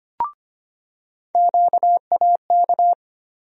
Morse code, heard as a keyed tone, about 25 wpm, 700 Hz. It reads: QAK
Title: USA Call Sign Postfixes - 3 Letters 25wpm